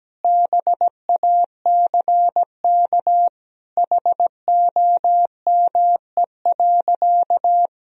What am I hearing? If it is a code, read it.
Morse code, heard as a keyed tone, about 17 wpm, 700 Hz. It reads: BACK HOME.
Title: Sentences from Top 500 Words and Abbreviations 17wpm